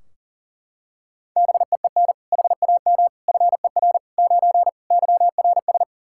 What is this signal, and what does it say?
Morse code, heard as a keyed tone, about 40 wpm, 700 Hz. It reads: BEEN HAM FER 9 YRS
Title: Practice QSO - Exchange 3 - More Info - 40wpm